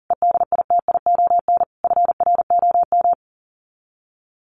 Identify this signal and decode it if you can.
Morse code, heard as a keyed tone, about 40 wpm, 700 Hz. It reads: EDITION FROM
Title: Sets of 5 Words Encore2 40wpm